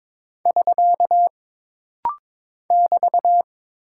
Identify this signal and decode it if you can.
Morse code, heard as a keyed tone, about 22 wpm, 700 Hz. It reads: <SK> <BT>
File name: practice / 22wpm / Single Punctuation-Prosign - Rapid-Fire - 22wpm